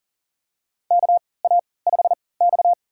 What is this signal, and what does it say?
Morse code, heard as a keyed tone, about 40 wpm, spaced as for 23 wpm, 700 Hz. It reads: KA5X
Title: USA Call Signs - 4 Characters - 3x Character Spacing - 40wpm